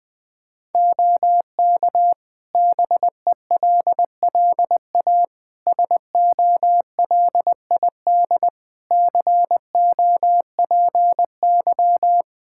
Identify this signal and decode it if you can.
Morse code, heard as a keyed tone, about 21 wpm, 700 Hz. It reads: OK BELLA SOLID COPY
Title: K1USN SST QSO Practice 20wpm